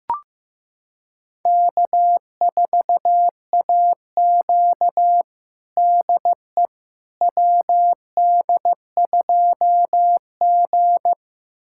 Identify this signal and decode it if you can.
Morse code, heard as a keyed tone, about 16 wpm, 700 Hz. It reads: K4AQ DE WD2G
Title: USA Callsign DE Callsign - 4 Characters - 15wpm